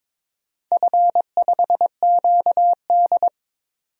Morse code, heard as a keyed tone, about 22 wpm, 700 Hz. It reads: F5QD
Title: DXCC-100 Call Signs - 4 Characters - 22wpm